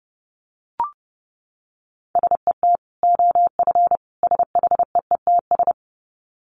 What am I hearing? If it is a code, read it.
Morse code, heard as a keyed tone, about 30 wpm, 700 Hz. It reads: SET OF SHEETS
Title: Dit Challenge 30wpm